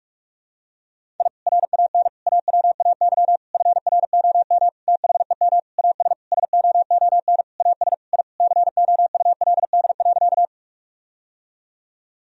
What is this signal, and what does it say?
Morse code, heard as a keyed tone, about 41 wpm, 700 Hz. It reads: I RAN AWAY FROM THEM AS SOON AS I COULD.
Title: Sentences from Top 1000 Words - Review Part 3 - 45wpm